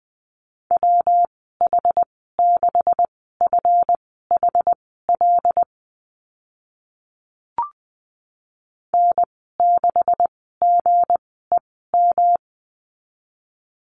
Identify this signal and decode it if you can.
Morse code, heard as a keyed tone, about 20 wpm, spaced as for 14 wpm, 700 Hz. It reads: WH6FHL N6GEM
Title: 161 Lesson 26g - Letter g - Call Signs 20wpm